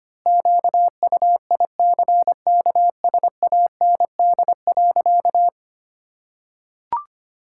Morse code, heard as a keyed tone, about 25 wpm, 700 Hz. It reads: QUICKSAND.
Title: Pangram Challenge v2 - 25wpm